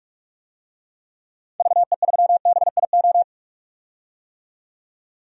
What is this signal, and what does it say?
Morse code, heard as a keyed tone, about 47 wpm, 700 Hz. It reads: VE3BIO
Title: DXCC-100 Call Signs - 6 Characters - 45wpm